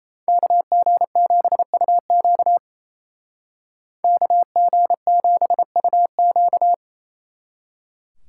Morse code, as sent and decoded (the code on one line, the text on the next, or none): -.- --. --... ..- --.- / -.- --. --... ..- --.-
KG7UQ KG7UQ